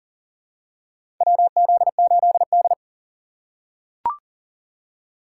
Morse code, heard as a keyed tone, about 40 wpm, 700 Hz. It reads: WZ8D
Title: USA Call Signs - 4x Character Spacing - 40wpm